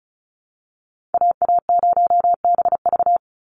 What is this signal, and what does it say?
Morse code, heard as a keyed tone, about 34 wpm, 700 Hz. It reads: AA0BV
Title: NA SOTA Activator Callsigns 35wpm